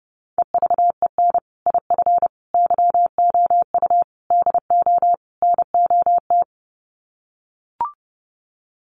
Morse code, heard as a keyed tone, about 30 wpm, 700 Hz. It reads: EVEN IF YOU DO NOT
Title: Sets of 5 Words Encore 30wpm